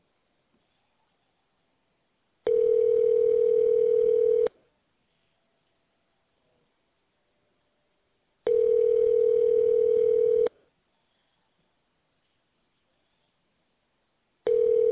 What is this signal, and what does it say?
Ringback tone (North America), 440 + 480 Hz.